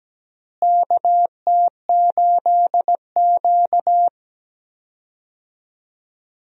Morse code, heard as a keyed tone, about 17 wpm, 700 Hz. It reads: KT8Q
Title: USA Call Signs - 4 Characters - Repeated 2x - 17wpm